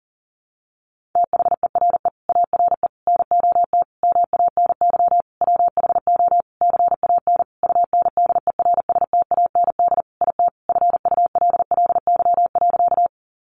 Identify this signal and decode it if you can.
Morse code, heard as a keyed tone, about 40 wpm, 700 Hz. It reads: THERE ARE NOT MANY WHO CAN UNDERSTAND IT FULLY.
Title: Sentences from Top 1300 Words 40wpm